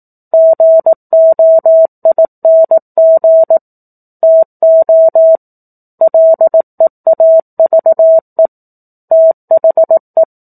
Morse code, heard as a keyed tone, about 18 wpm, 650 Hz. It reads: GOING TO LEAVE THE
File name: 18 WPM